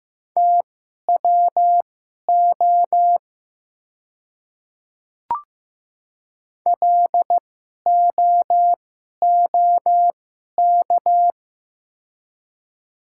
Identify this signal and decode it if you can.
Morse code, heard as a keyed tone, about 15 wpm, spaced as for 11 wpm, 700 Hz. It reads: TWO LOOK
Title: 2x Character Spacing - Top 200 Words - 15wpm